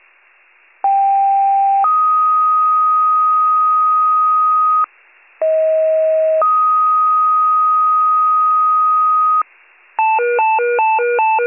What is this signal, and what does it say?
Two-tone page, heard as a one-second tone, then a three-second tone, about 790 then 1250 Hz, about 640 then 1250 Hz. It is heard from 0.8 to 9.4 s.